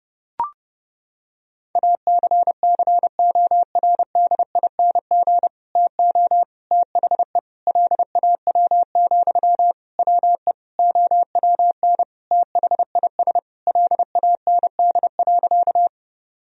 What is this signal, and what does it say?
Morse code, heard as a keyed tone, about 32 wpm, 700 Hz. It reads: ACCORDING TO THE LAW, WE OWN THIS LAND.
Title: Sentences from Top 1000 Words - Review Part 4 - 30wpm